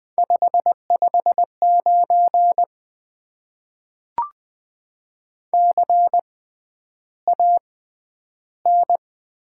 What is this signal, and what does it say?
Morse code, heard as a keyed tone, about 20 wpm, spaced as for 10 wpm, 700 Hz. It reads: 559 CAN
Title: Top 90 QSO Elements - 6x Character Spacing - 20wpm